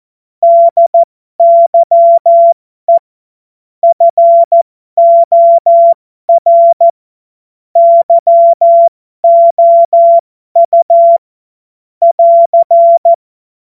Morse code, heard as a keyed tone, about 14 wpm, spaced as for 12 wpm, 680 Hz. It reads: DYE FOR YOU <AR>
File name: Morsekurs / Witze